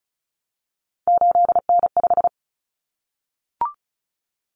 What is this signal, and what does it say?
Morse code, heard as a keyed tone, about 34 wpm, 700 Hz. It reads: 8N5
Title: Common Cut Numbers - 3-Digit 35wpm